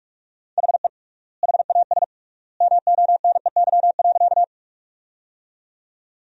Morse code, heard as a keyed tone, about 41 wpm, 700 Hz. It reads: HE HAS MONEY.
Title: 3x Word-Spacing - Sentences from Top 400 Words - 45wpm